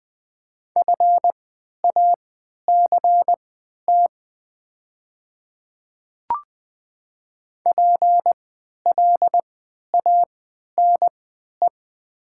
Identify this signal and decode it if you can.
Morse code, heard as a keyed tone, about 20 wpm, spaced as for 11 wpm, 700 Hz. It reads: FACT PLANE
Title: 3x Character Spacing - Top 500 Words - 20wpm